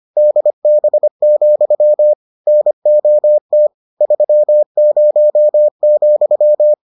Morse code, heard as a keyed tone, about 25 wpm, 600 Hz. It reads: DB, NOT 30,